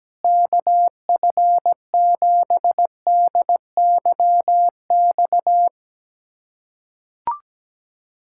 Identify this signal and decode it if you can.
Morse code, heard as a keyed tone, about 17 wpm, 700 Hz. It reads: KF7DYX
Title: K1USN SST QSO Practice - 17wpm